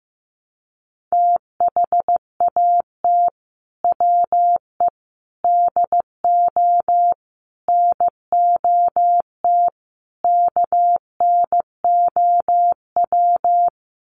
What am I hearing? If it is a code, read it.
Morse code, heard as a keyed tone, about 15 wpm, 700 Hz. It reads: THAT WE DO NOT KNOW
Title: Sets of 5 Words Encore2 15wpm